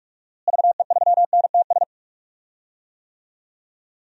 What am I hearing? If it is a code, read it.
Morse code, heard as a keyed tone, about 41 wpm, 700 Hz. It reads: VE3NTS
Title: DXCC-100 Call Signs - 6 Characters - Repeated 3x - 45wpm